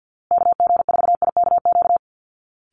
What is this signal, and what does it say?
Morse code, heard as a keyed tone, about 50 wpm, 700 Hz. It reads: KG4IKQ
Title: Call Signs Repeated 2x - 50wpm